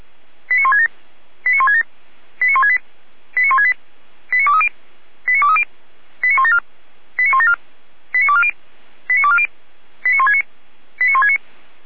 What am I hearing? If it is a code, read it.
Five-tone selective call (ZVEI), heard as quick runs of five tones, tones of about 70 ms. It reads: 89167 89167 89168 89168 89230 89230 89174 89174 89240 89240 89169 89169